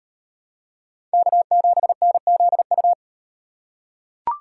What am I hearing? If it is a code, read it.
Morse code, heard as a keyed tone, about 38 wpm, 700 Hz. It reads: K7NZU